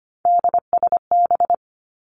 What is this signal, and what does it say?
Morse code, heard as a keyed tone, about 25 wpm, 700 Hz. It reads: DSB
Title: Three Challenging Letter-Number Combinations 25wpm